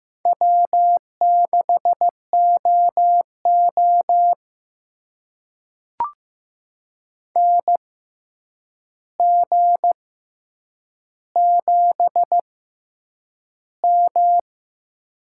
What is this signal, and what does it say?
Morse code, heard as a keyed tone, about 15 wpm, spaced as for 8 wpm, 700 Hz. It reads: W6OO NG7M
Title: USA Call Signs - 4 Characters - 6x Character Spacing - 15wpm